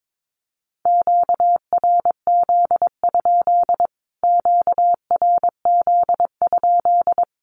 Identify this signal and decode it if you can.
Morse code, heard as a keyed tone, about 22 wpm, 700 Hz. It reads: QRZ? QRZ?
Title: Q-Codes Repeated 3x 22wpm